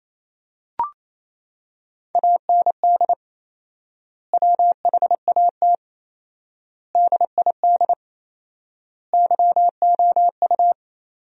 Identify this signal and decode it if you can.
Morse code, heard as a keyed tone, about 28 wpm, 700 Hz. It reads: AND WHAT DID YOU
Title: 4x Word-Spacing - Sentences from Top 100 Words - 28wpm